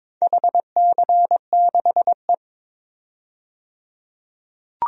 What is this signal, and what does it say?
Morse code, heard as a keyed tone, about 22 wpm, 700 Hz. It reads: HC6E
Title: DXCC-100 Call Signs - 4 Characters - 22wpm